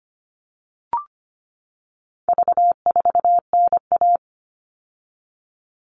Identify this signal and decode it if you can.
Morse code, heard as a keyed tone, about 25 wpm, 700 Hz. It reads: V4NA
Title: All Cut Numbers - 4-Digit 25wpm